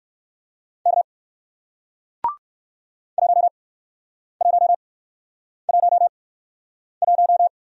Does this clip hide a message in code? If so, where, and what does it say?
Yes: Morse code from 0.9 to 7.5 s, heard as a keyed tone, about 57 wpm, spaced as for 9 wpm, 700 Hz. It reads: K 1111